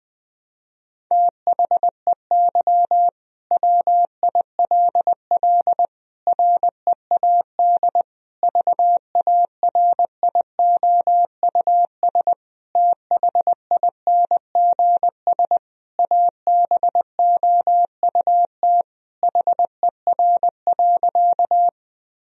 Morse code, heard as a keyed tone, about 20 wpm, 700 Hz. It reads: THEY WILL READ VARIOUS THINGS ABOUT HER.
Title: Sentences from Top 1000 Words - Review Part 4 - 20wpm